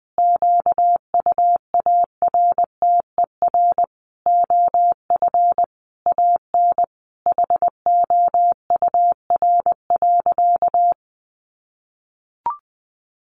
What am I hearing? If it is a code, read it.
Morse code, heard as a keyed tone, about 20 wpm, 700 Hz. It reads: QUARTER OF AN HOUR.
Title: Sentences from Top 1300 Words 20wpm